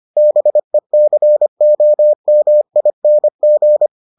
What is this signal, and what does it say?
Morse code, heard as a keyed tone, about 25 wpm, 600 Hz. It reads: BECOMING